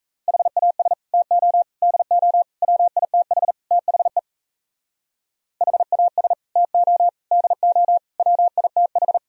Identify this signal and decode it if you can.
Morse code, heard as a keyed tone, about 36 wpm, 700 Hz. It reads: HAS TO DO WITH THE HAS TO DO WITH